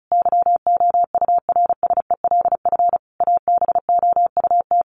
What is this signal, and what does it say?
Morse code, heard as a keyed tone, about 34 wpm, 700 Hz. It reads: YOURSELF ABOUT
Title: Sentences from Top 1200 Words 35wpm